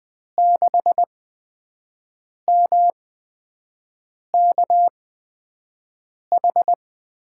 Morse code, heard as a keyed tone, about 20 wpm, spaced as for 5 wpm, 700 Hz. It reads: 6MKH
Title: USA Call Signs - 6 Characters - 8x Character Spacing - 20wpm